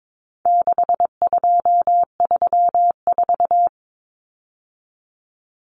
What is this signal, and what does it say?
Morse code, heard as a keyed tone, about 22 wpm, 700 Hz. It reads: 6234
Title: Common Cut Numbers - 4-Digit 22wpm